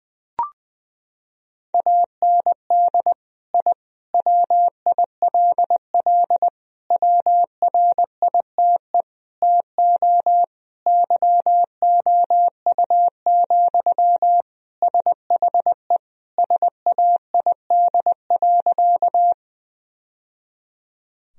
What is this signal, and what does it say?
Morse code, heard as a keyed tone, about 20 wpm, 700 Hz. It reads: AND I WILL WRITE TO YOU, SHE SAID.
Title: Sentences from Top 1000 Words - Review Part 1 - 20wpm